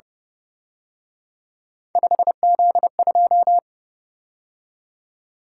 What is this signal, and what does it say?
Morse code, heard as a keyed tone, about 29 wpm, 700 Hz. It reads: 5Z2
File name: Speed-Racing - Three Letter-Number Combinations - 25wpm